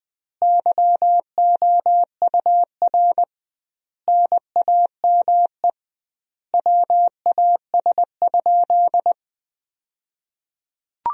Morse code, heard as a keyed tone, about 21 wpm, 700 Hz. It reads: YOUR NAME WAS?
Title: 2x Word-Spacing - Sentences from Top 200 Words - 20wpm